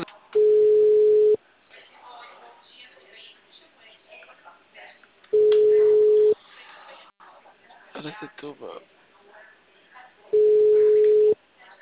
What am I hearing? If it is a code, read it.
Ringback tone (Europe, 425 Hz), with speech.